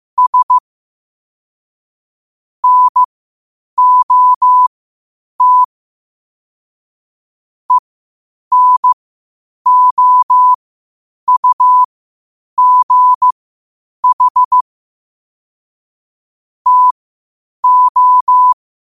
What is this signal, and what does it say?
Morse code, heard as a keyed tone, about 15 wpm, spaced as for 8 wpm, 1000 Hz. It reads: S NOT ENOUGH TO
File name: qotdfiles / 8wpm